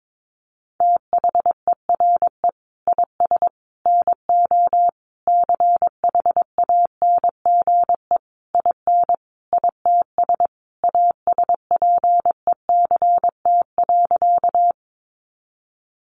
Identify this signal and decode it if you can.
Morse code, heard as a keyed tone, about 22 wpm, 700 Hz. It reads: THERE IS NO CHANGE IN ITS ASPECT.
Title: Sentences from Top 1300 Words 22wpm